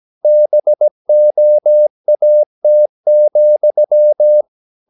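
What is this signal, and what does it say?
Morse code, heard as a keyed tone, about 17 wpm, 600 Hz. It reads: BOAT,
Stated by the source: 17 wpm Morse practice sending the English text of a novel.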